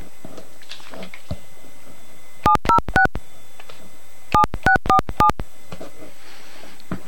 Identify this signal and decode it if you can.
DTMF, heard as telephone touch tones, tones of about 100 ms. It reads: 706 7647